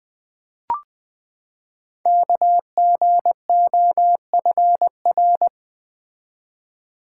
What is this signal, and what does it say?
Morse code, heard as a keyed tone, about 20 wpm, 700 Hz. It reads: KGOFR
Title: Code Groups - 5 Letters - 20wpm